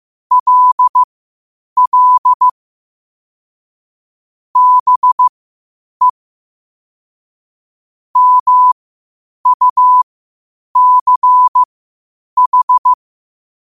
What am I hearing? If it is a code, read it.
Morse code, heard as a keyed tone, about 15 wpm, spaced as for 8 wpm, 1000 Hz. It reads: LL BE MUCH
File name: qotdfiles / 8wpm